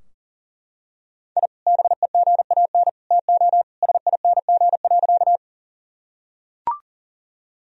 Morse code, heard as a keyed tone, about 40 wpm, 700 Hz. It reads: I BEGAN TO SING.
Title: Sentences from Top 1000 Words - Review Part 2 - 40wpm